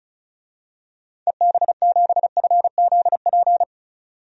Morse code, heard as a keyed tone, about 36 wpm, 700 Hz. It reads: EB7FZP
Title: DXCC-100 Call Signs - 6 Characters - 35wpm